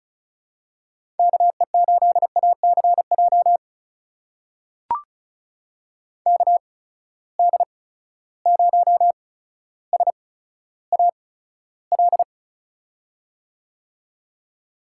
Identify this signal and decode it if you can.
Morse code, heard as a keyed tone, about 36 wpm, spaced as for 15 wpm, 700 Hz. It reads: KE8ACJ KD0SAL
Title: USA Call Signs - 6 Characters - 8x Character Spacing - 35wpm